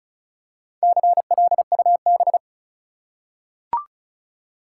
Morse code, heard as a keyed tone, about 36 wpm, 700 Hz. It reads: CLUB
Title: Four Letter Words - 35wpm